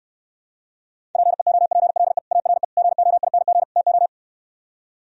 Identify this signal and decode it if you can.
Morse code, heard as a keyed tone, about 68 wpm, 700 Hz. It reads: PEOPLE ARE GOING TO